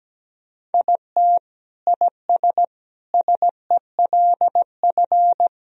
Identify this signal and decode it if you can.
Morse code, heard as a keyed tone, about 17 wpm, 700 Hz. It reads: IT IS SELF
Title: Sentences from Top 1000 Words - Review Part 1 - 17wpm